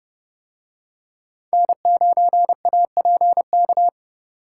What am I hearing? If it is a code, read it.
Morse code, heard as a keyed tone, about 28 wpm, 700 Hz. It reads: N9APK